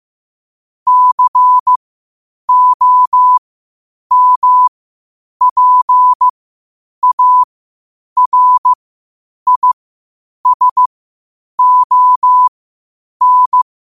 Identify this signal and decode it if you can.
Morse code, heard as a keyed tone, about 15 wpm, spaced as for 8 wpm, 1000 Hz. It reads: COMPARISON
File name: qotdfiles / 8wpm